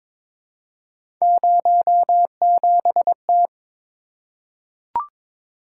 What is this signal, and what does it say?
Morse code, heard as a keyed tone, about 23 wpm, 700 Hz. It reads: 07T
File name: practice / 22wpm / Three Letter-Number Combinations - 22wpm